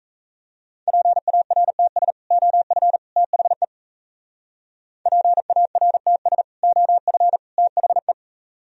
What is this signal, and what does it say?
Morse code, heard as a keyed tone, about 33 wpm, 700 Hz. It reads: PARTS OF THE PARTS OF THE
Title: Speed-Racing - Sets of 3 Words 28wpm